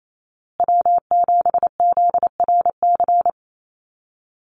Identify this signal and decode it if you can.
Morse code, heard as a keyed tone, about 28 wpm, 700 Hz. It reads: W7ZRC
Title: Call Signs Repeated 3x - 28wpm